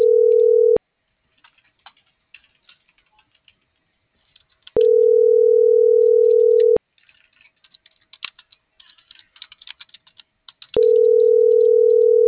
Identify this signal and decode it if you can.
Ringback tone (North America), 440 + 480 Hz.